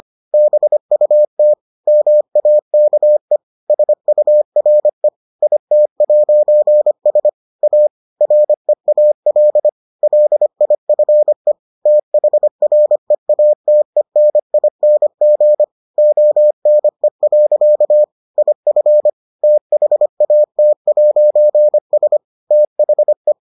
Morse code, heard as a keyed tone, about 25 wpm, 600 Hz. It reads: BUT MAKE SURE IT'S A REAL LIFE THREATENING ONE. IF THAT'S THE